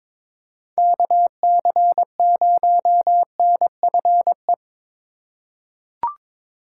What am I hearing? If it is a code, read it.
Morse code, heard as a keyed tone, about 22 wpm, 700 Hz. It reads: KC0NFE
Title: USA Call Signs - 6 Characters - 2x Character Spacing - 22wpm